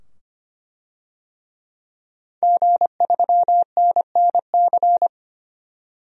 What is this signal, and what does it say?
Morse code, heard as a keyed tone, about 25 wpm, 700 Hz. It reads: G3NNC
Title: DXCC-100 Call Signs - 5 Characters - 25wpm